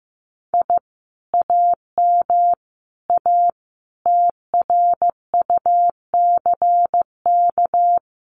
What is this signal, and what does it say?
Morse code, heard as a keyed tone, about 15 wpm, 700 Hz. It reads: I AM A TRUCK
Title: Common QSO Phrases Encore 15wpm